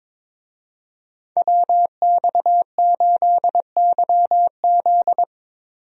Morse code, heard as a keyed tone, about 22 wpm, 700 Hz. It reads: WX8YZ
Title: USA Call Signs - 5 Characters - Repeated 2x - 22wpm